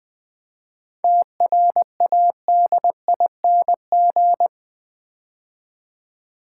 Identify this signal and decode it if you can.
Morse code, heard as a keyed tone, about 20 wpm, 690 Hz. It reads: TRADING
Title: 7 Letter Words - First 3 Letters Given - 20wpm